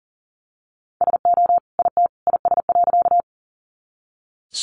Morse code, heard as a keyed tone, about 40 wpm, 700 Hz. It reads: SO IT IS.
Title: Sentences from Top 100 Words 40wpm